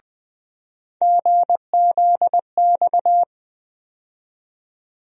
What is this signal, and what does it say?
Morse code, heard as a keyed tone, about 20 wpm, 700 Hz. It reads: GZX
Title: USA Call Sign Postfixes - 3 Letters 20wpm